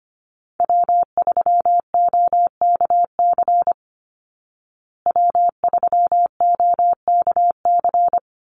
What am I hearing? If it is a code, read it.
Morse code, heard as a keyed tone, about 25 wpm, 700 Hz. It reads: W3OKC W3OKC
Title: Call Signs Repeated 2x - 25wpm